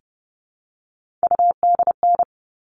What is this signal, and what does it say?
Morse code, heard as a keyed tone, about 30 wpm, 700 Hz. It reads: UDN
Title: All Cut Numbers - 3-Digit 30wpm